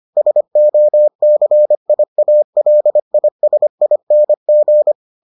Morse code, heard as a keyed tone, about 25 wpm, 600 Hz. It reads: SOCIALISING